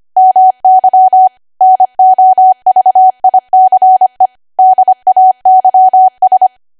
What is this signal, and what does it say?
Morse code, heard as a keyed tone, about 25 wpm, 750 Hz. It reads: MY NOVICE DAYS